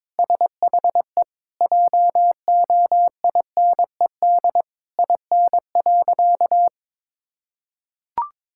Morse code, heard as a keyed tone, about 22 wpm, 700 Hz. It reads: SHE JOINED IN.
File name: practice / 22wpm / Sentences with 2-Word Phrases - 22wpm